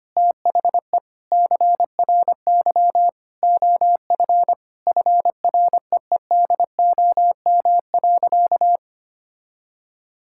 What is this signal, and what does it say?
Morse code, heard as a keyed tone, about 25 wpm, 700 Hz. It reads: THE CRY OF FREEDOM.
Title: Sentences from Top 1000 Words - Review Part 5 - 25wpm